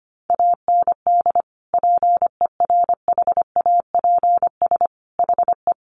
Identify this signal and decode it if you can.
Morse code, heard as a keyed tone, about 25 wpm, 700 Hz. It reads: AND PERHAPS HE
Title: Sentences from Top 900 Words 25wpm